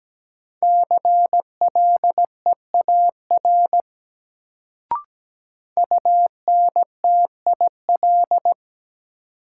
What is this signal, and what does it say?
Morse code, heard as a keyed tone, about 17 wpm, 700 Hz. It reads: CLEAR UNTIL
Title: Sets of 5 Words Encore - New Word Drill - 17wpm